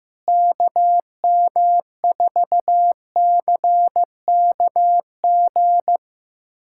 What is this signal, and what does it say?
Morse code, heard as a keyed tone, about 15 wpm, 700 Hz. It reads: KM4CKG
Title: USA Callsign DE Callsign 15wpm